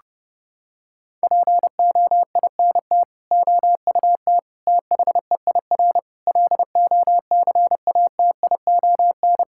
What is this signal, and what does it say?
Morse code, heard as a keyed tone, about 30 wpm, 700 Hz. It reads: POINT OUT THEIR LOCATION
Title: Sentences from Top 1000 Words - Review Part 6 - 30wpm